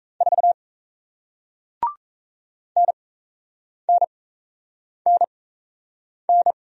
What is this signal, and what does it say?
Morse code, heard as a keyed tone, about 37 wpm, spaced as for 8 wpm, 700 Hz. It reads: 4 NNNN